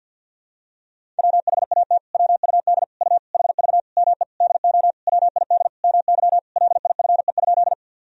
Morse code, heard as a keyed tone, about 50 wpm, 700 Hz. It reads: WHAT WUD U HV ME DO WID MY LIFE?
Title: Sentences from Top 300 Words and Abbreviations 50wpm